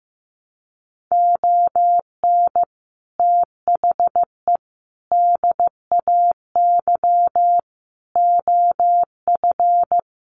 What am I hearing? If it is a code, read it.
Morse code, heard as a keyed tone, about 15 wpm, 700 Hz. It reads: ON THE DAY OF
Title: Sets of 5 Words Encore 15wpm